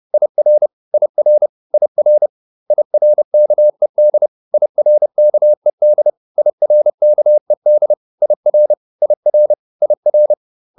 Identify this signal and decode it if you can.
Morse code, heard as a keyed tone, about 30 wpm, 600 Hz. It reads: IR IR IR IRKED IRKED IRKED IR IR IR